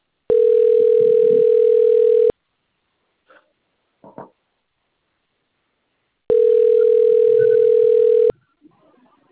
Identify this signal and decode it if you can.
Ringback tone (North America), 440 + 480 Hz.
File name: rg-1600-unknown- - - .21148